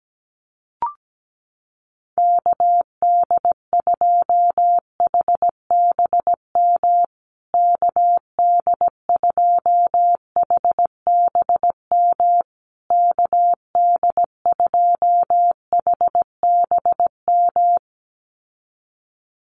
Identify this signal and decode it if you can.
Morse code, heard as a keyed tone, about 17 wpm, 700 Hz. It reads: KD2HBM KD2HBM KD2HBM
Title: USA Call Signs Repeated 3x 17wpm